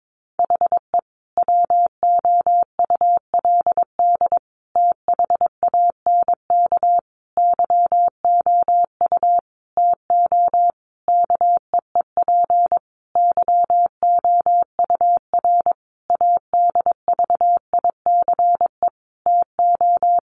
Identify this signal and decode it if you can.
Morse code, heard as a keyed tone, about 22 wpm, 700 Hz. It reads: HE WOULD THANK YOU TO KEEP YOUR ADVICE TO
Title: Sentences from Top 1100 Words 22wpm